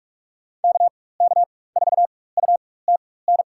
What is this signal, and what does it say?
Morse code, heard as a keyed tone, about 46 wpm, spaced as for 21 wpm, 700 Hz. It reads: KK4UTN
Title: USA Call Signs - 6 Characters - 4x Character Spacing - 45wpm